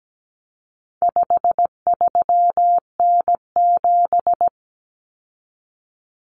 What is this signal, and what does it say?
Morse code, heard as a keyed tone, about 17 wpm, 700 Hz. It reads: 53N7
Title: Common Cut Numbers - 4-Digit 17wpm